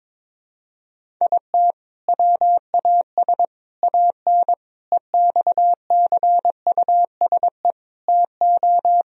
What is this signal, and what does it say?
Morse code, heard as a keyed tone, about 22 wpm, 700 Hz. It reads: IT WAS AN EXCUSE TO
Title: Sentences from Top 1000 Words - Review Part 1 - 22wpm